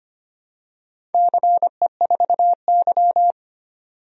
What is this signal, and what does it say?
Morse code, heard as a keyed tone, about 25 wpm, 700 Hz. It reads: CE4Y
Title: DXCC-100 Call Signs - 4 Characters - 25wpm